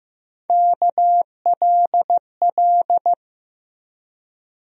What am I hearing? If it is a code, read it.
Morse code, heard as a keyed tone, about 15 wpm, 700 Hz. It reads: KLL
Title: USA Call Sign Postfixes - 3 Letters 15wpm